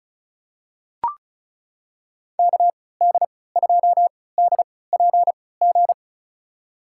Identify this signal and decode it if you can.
Morse code, heard as a keyed tone, about 36 wpm, spaced as for 20 wpm, 700 Hz. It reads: KD2DPG
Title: USA Call Signs - 6 Characters - 3x Character Spacing - 35wpm